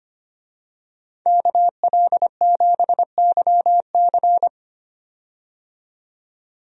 Morse code, heard as a keyed tone, about 25 wpm, 700 Hz. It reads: KL7YC
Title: DXCC-100 Call Signs - 5 Characters - 25wpm